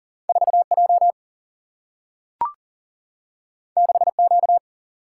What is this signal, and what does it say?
Morse code, heard as a keyed tone, about 40 wpm, 700 Hz. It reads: 4J 6Q
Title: Single-Letter Single-Number 40wpm